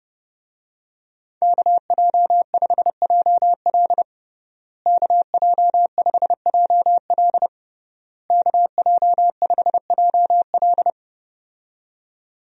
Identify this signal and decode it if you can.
Morse code, heard as a keyed tone, about 32 wpm, 700 Hz. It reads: KJ5JL KJ5JL KJ5JL
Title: USA Call Signs - 5 Characters - Repeated 3x - 30wpm